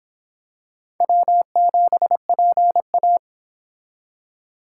Morse code, heard as a keyed tone, about 25 wpm, 700 Hz. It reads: W7PA